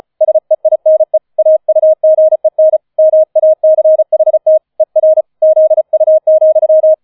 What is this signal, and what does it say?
Morse code, heard as a keyed tone, about 36 wpm, 630 Hz. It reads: SEINE AUGEN MACHT ER ZU,
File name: 005 Fünfter Streich 35wpm